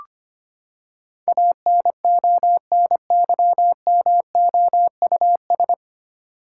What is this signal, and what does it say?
Morse code, heard as a keyed tone, about 25 wpm, 700 Hz. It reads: ANONYMOUS
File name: Nine Letter Words - 25wpm